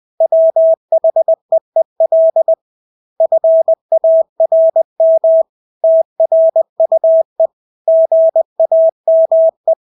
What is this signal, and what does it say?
Morse code, heard as a keyed tone, about 20 wpm, 650 Hz. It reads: WHEEL FARM TRUE GAME
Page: Engelske ord 20 wpm